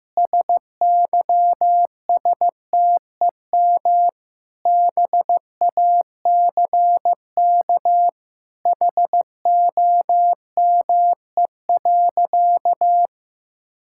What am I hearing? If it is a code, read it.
Morse code, heard as a keyed tone, about 15 wpm, 700 Hz. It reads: SYSTEM BACK HOME.
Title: Sentences from Top 1000 Words - Review Part 5 - 15wpm